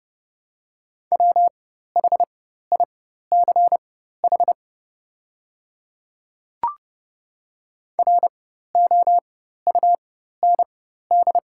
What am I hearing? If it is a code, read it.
Morse code, heard as a keyed tone, about 31 wpm, spaced as for 14 wpm, 700 Hz. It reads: WHICH ROUND
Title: 4x Character Spacing - Top 300 Words - 30wpm